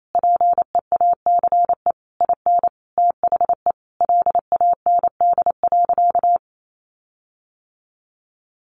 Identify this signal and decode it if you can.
Morse code, heard as a keyed tone, about 28 wpm, 700 Hz. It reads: PEACE IN THE LAND.
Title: Sentences from Top 1500 Words 28wpm